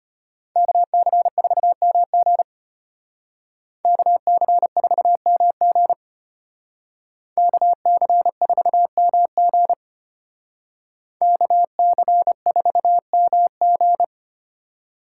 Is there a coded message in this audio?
Yes: Morse code from 0.6 to 14.0 s, heard as a keyed tone, about 32 wpm, 700 Hz. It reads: KC4MG KC4MG KC4MG KC4MG